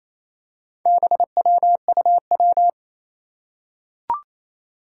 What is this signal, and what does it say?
Morse code, heard as a keyed tone, about 28 wpm, 700 Hz. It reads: BWUW
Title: Four Challenging Letter-Number Combinations 28wpm